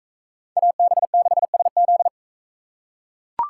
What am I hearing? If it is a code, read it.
Morse code, heard as a keyed tone, about 41 wpm, 700 Hz. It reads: AB6SZ